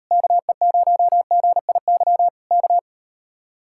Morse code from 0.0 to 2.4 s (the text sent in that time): KE0GIY